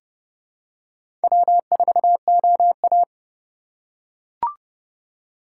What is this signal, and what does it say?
Morse code, heard as a keyed tone, about 30 wpm, 700 Hz. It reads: W4OA
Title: USA Call Signs - 4 Characters - 3x Character Spacing - 30wpm